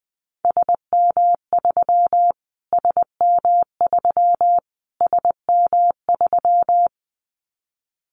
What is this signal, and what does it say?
Morse code, heard as a keyed tone, about 20 wpm, 700 Hz. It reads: SM3 SM3 SM3
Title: Callsign Prefixes for Top 25 DXCC - Repeated 3x - 20wpm